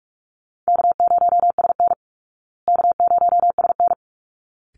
Morse code, heard as a keyed tone, about 44 wpm, 700 Hz. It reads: K0SN K0SN
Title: Easy Call Signs Repeated 2x - 45wpm